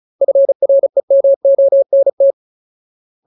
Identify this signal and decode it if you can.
Morse code, heard as a keyed tone, about 34 wpm, 550 Hz. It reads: FREMONT
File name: Ultimate Word List / 35WPM Ultimate